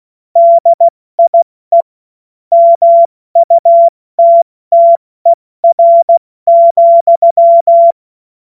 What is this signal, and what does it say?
Morse code, heard as a keyed tone, about 16 wpm, 680 Hz. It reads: DIE MUTTER,
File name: Morsekurs / Witze / Witz 23. August 2022 70 F80